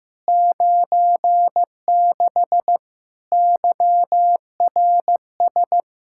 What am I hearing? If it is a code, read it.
Morse code, heard as a keyed tone, about 15 wpm, 700 Hz. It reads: 96 YRS
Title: Practice QSO - Exchange 3 - More Info - 15wpm